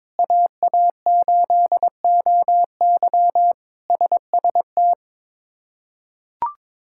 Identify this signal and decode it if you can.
Morse code, heard as a keyed tone, about 22 wpm, 700 Hz. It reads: AA8OY SST
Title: K1USN SST QSO Practice - 22wpm